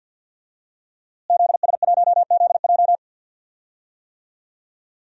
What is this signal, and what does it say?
Morse code, heard as a keyed tone, about 51 wpm, 700 Hz. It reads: ZS1ZJ
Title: DXCC-100 Call Signs - 5 Characters - Repeated 3x - 50wpm